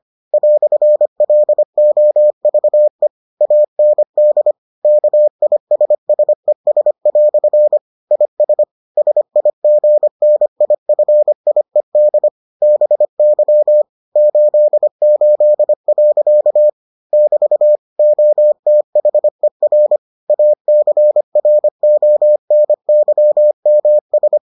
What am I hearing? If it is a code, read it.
Morse code, heard as a keyed tone, about 25 wpm, 600 Hz. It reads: "LOVE AND KISSES" IS SIGNIFIED BY 88. <BT> OTHER ACRONYMS